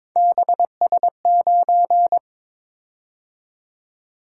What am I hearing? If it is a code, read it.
Morse code, heard as a keyed tone, about 22 wpm, 700 Hz. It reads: BS9
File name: practice / 22wpm / Three Letter-Number Combinations - 22wpm